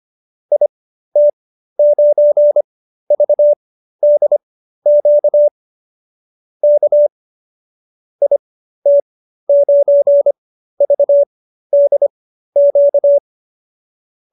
Morse code, heard as a keyed tone, about 25 wpm, spaced as for 13 wpm, 590 Hz. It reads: IT9VDQ K IT9VDQ